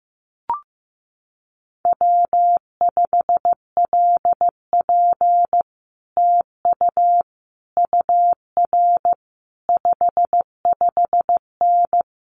Morse code, heard as a keyed tone, about 15 wpm, 700 Hz. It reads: W5LP TU UR 55N
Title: Common QSO Phrases Encore 15wpm